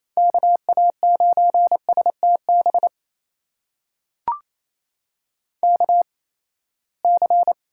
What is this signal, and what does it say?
Morse code, heard as a keyed tone, about 27 wpm, spaced as for 19 wpm, 700 Hz. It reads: KA9STB KC
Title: USA Call Signs - 6 Characters - 8x Character Spacing - 28wpm